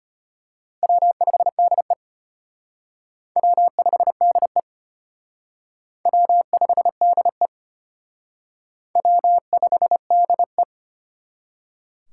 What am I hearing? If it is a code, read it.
Morse code, heard as a keyed tone, about 31 wpm, 700 Hz. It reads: W5DE W5DE W5DE W5DE
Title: Speed Racing - USA Call Signs 25wpm